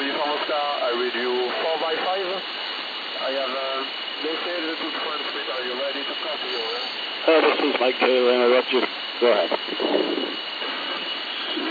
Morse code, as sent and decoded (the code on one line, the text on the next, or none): none